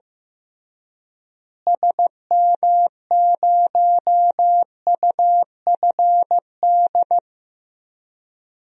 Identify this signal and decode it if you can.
Morse code, heard as a keyed tone, about 16 wpm, 700 Hz. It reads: SM0UFD
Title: DXCC-100 Call Signs - 6 Characters - Repeated 3x - 15wpm